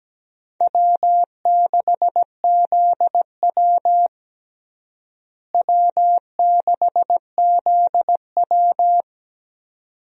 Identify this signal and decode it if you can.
Morse code, heard as a keyed tone, about 17 wpm, 700 Hz. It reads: W6ZW W6ZW
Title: USA Call Signs - 4 Characters - Repeated 2x - 17wpm